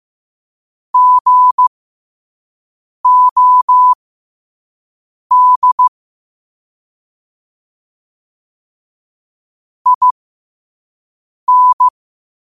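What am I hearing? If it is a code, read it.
Morse code, heard as a keyed tone, about 15 wpm, spaced as for 5 wpm, 1000 Hz. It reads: GOD IN
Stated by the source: Morse code quote of the day at 5 wpm.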